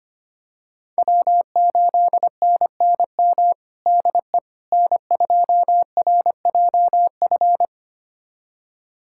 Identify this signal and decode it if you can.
Morse code, heard as a keyed tone, about 25 wpm, 700 Hz. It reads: W8NNM DE N2RJF
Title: USA Callsign DE Callsign - 5 Characters - 25wpm